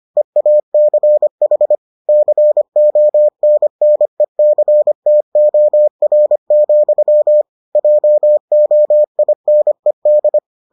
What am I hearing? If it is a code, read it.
Morse code, heard as a keyed tone, about 25 wpm, 600 Hz. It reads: EACH CONNECTOR, JOINED